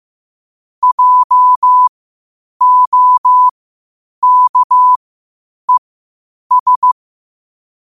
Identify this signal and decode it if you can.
Morse code, heard as a keyed tone, about 15 wpm, spaced as for 8 wpm, 1000 Hz. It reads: JOKES
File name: qotdfiles / 8wpm